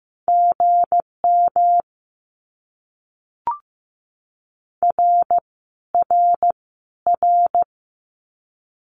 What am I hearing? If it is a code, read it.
Morse code, heard as a keyed tone, about 15 wpm, 700 Hz. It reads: GM R R R
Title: Abbreviations Repeated 3x - 15wpm